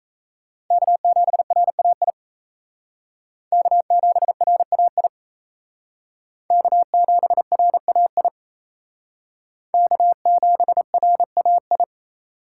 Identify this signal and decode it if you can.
Morse code, heard as a keyed tone, about 36 wpm, 700 Hz. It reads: K7RAI K7RAI K7RAI K7RAI